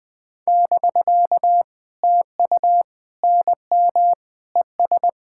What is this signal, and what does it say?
Morse code, heard as a keyed tone, about 20 wpm, 700 Hz. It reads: <BK> TU NM ES
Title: POTA QSO Practice - 20wpm